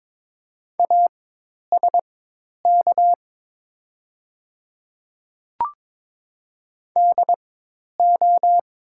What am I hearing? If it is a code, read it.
Morse code, heard as a keyed tone, about 23 wpm, spaced as for 10 wpm, 700 Hz. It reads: ASK DO
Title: 4x Character Spacing - Top 200 Words - 22wpm